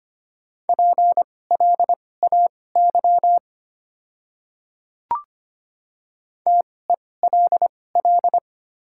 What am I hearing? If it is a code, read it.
Morse code, heard as a keyed tone, about 25 wpm, spaced as for 18 wpm, 700 Hz. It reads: PLAY TELL
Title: 2x Character Spacing - Top 500 Words - 25wpm